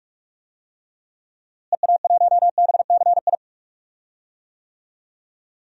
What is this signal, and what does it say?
Morse code, heard as a keyed tone, about 46 wpm, 700 Hz. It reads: EA1BCI
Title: DXCC-100 Call Signs - 6 Characters - Repeated 3x - 45wpm